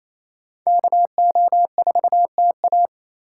Morse code, heard as a keyed tone, about 38 wpm, spaced as for 33 wpm, 700 Hz. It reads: KO4TA
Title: Speed Racing - USA Call Signs 28wpm